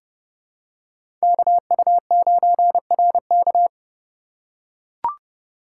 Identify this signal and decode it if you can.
Morse code, heard as a keyed tone, about 30 wpm, 700 Hz. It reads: KU9RK
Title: K1USN SST QSO Practice - 30wpm